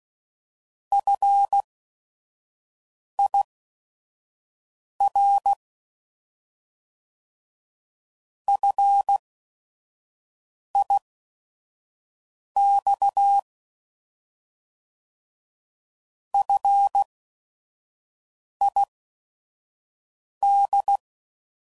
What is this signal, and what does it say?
Morse code, heard as a keyed tone, about 16 wpm, spaced as for 5 wpm, 800 Hz. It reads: FIR FIX FID
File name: Prefixes / By First Letter / Fs